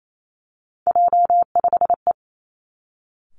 Morse code, heard as a keyed tone, about 28 wpm, 700 Hz. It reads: J5E
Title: Three Challenging Letter-Number Combinations 28wpm